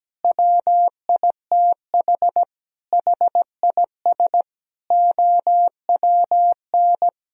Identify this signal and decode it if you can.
Morse code, heard as a keyed tone, about 17 wpm, 700 Hz. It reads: WITH HIS OWN
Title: Sentences from Top 1000 Words - Review Part 1 - 17wpm